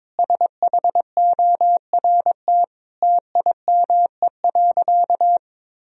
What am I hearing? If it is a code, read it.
Morse code, heard as a keyed tone, about 22 wpm, 700 Hz. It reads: SHORT TIME.
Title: Sentences from Top 1000 Words - Review Part 3 - 22wpm